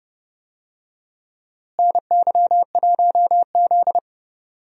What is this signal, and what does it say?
Morse code, heard as a keyed tone, about 32 wpm, 700 Hz. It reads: NY1Z
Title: USA Call Signs - 4 Characters - Repeated 3x - 30wpm